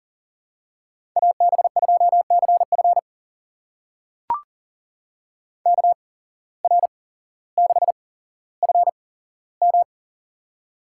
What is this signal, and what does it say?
Morse code, heard as a keyed tone, about 40 wpm, spaced as for 17 wpm, 700 Hz. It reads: AB2CF KR6FM
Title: USA Call Signs - 5 Characters - 8x Character Spacing - 40wpm